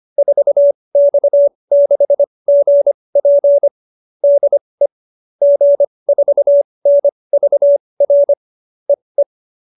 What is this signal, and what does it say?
Morse code, heard as a keyed tone, about 25 wpm, spaced as for 20 wpm, 570 Hz. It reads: 4X6GP DE G4NVR EE